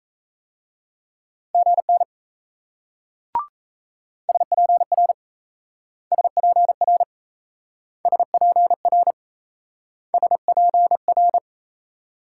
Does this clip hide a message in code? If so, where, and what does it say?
Yes: Morse code from 1.5 to 11.4 s, heard as a keyed tone, about 37 wpm, 700 Hz. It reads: GN SPR SPR SPR SPR